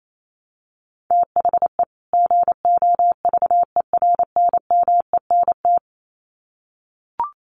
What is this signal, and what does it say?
Morse code, heard as a keyed tone, about 28 wpm, 700 Hz. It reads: THE GOVERNMENT
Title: Sets of 2 Words 28wpm